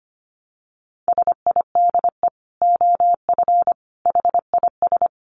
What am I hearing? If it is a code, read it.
Morse code, heard as a keyed tone, about 25 wpm, 700 Hz. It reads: SIDE OF HIS
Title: Dit Challenge 25wpm